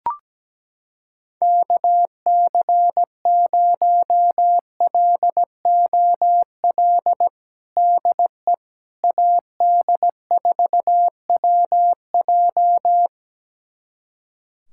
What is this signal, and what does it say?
Morse code, heard as a keyed tone, about 17 wpm, 700 Hz. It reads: KC0LOL DE AD4WJ
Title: USA Callsign DE Callsign 17wpm